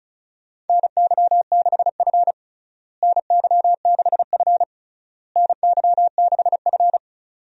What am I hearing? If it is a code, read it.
Morse code, heard as a keyed tone, about 36 wpm, 700 Hz. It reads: NY6F NY6F NY6F
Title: USA Call Signs - 4 Characters - Repeated 3x - 35wpm